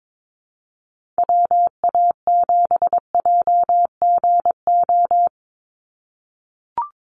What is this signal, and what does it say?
Morse code, heard as a keyed tone, about 22 wpm, 700 Hz. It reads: WA7JGO
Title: USA Call Signs 22wpm